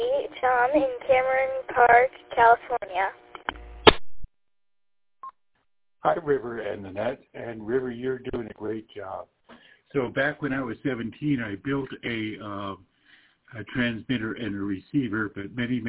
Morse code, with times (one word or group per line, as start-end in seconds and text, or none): none